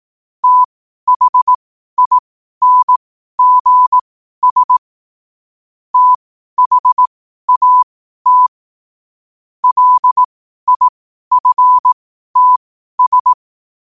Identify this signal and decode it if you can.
Morse code, heard as a keyed tone, about 18 wpm, spaced as for 12 wpm, 1000 Hz. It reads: THINGS THAT LIFTS
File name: qotdfiles / 12wpm